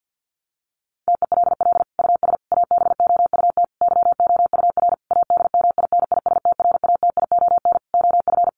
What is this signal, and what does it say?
Morse code, heard as a keyed tone, about 50 wpm, 700 Hz. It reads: TELL US ABOUT YOUR ADMINISTRATION OF
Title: Sentences from Top 1300 Words 50wpm